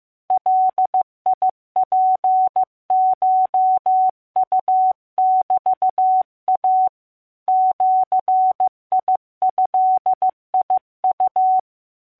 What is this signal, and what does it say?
Morse code (Japanese Wabun code), heard as a keyed tone, about 15 wpm, 750 Hz. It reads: ガツコウメイ ジドウ